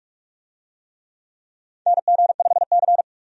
Morse code, heard as a keyed tone, about 46 wpm, 700 Hz. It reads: NG5C
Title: USA Call Signs - 4 Characters - 45wpm